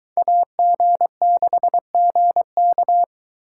Morse code, heard as a keyed tone, about 22 wpm, 700 Hz. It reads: AG6GK